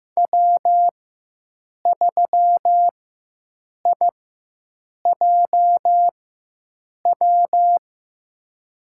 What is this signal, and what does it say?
Morse code, heard as a keyed tone, about 15 wpm, spaced as for 7 wpm, 700 Hz. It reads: W3IJW
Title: USA Call Signs - 5 Characters - 4x Character Spacing - 15wpm